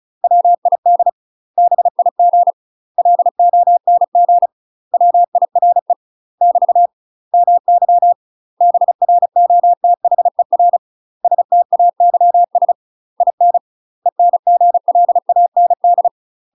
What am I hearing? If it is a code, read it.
Morse code, heard as a keyed tone, about 34 wpm, 700 Hz. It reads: WID BIG LONG WIRE <BT> MY BROTHER STAYS IN ENGLAND
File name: UK MASTER QSOs TRAINING/35WPM